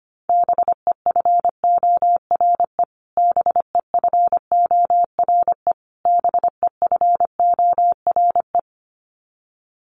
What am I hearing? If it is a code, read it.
Morse code, heard as a keyed tone, about 25 wpm, 700 Hz. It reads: BEFORE BEFORE BEFORE
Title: Top 500 Words Repeated 25wpm